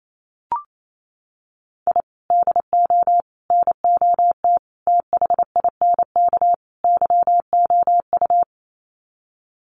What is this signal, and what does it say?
Morse code, heard as a keyed tone, about 28 wpm, 700 Hz. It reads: I DO NOT THINK YOU
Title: Sets of 5 Words 28wpm